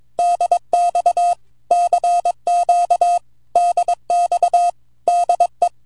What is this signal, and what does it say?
Morse code, heard as a keyed tone, about 22 wpm, 700 Hz. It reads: DX CQ DX DE
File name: vq9la / ok1fdy 10AUG08 0115Z 160mtr